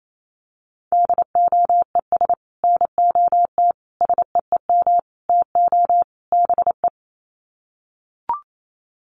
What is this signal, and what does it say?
Morse code, heard as a keyed tone, about 28 wpm, 700 Hz. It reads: DOES NOT SEEM TO BE
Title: Sets of 5 Words 28wpm